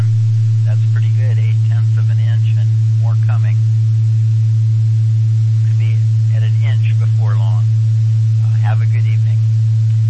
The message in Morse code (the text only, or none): none